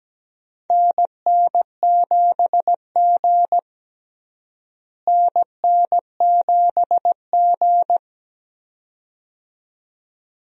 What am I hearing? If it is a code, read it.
Morse code, heard as a keyed tone, about 17 wpm, 700 Hz. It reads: NN7G NN7G
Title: USA Call Signs - 4 Characters - Repeated 2x - 17wpm